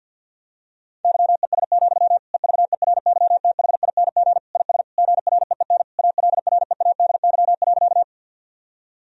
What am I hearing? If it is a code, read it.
Morse code, heard as a keyed tone, about 52 wpm, 700 Hz. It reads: YES, EVERYTHING IS GREEN ALREADY.
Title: Sentences from Top 1000 Words - Review Part 2 - 50wpm